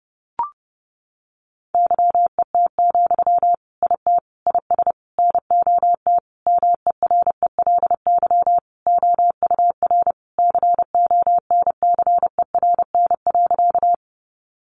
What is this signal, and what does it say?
Morse code, heard as a keyed tone, about 30 wpm, 700 Hz. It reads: YET, IT IS NOT MERELY OUR CONCERN.
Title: Sentences from Top 1400 Words 30wpm